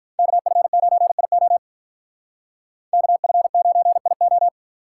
Morse code, heard as a keyed tone, about 45 wpm, 700 Hz. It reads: KF9IO KF9IO